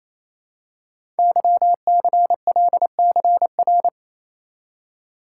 Morse code, heard as a keyed tone, about 27 wpm, 700 Hz. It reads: YCLCR
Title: Code Groups - 5 Letters - 28wpm